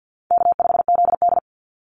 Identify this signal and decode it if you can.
Morse code, heard as a keyed tone, about 50 wpm, 700 Hz. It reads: K5ZD
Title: Easy Call Signs - 50wpm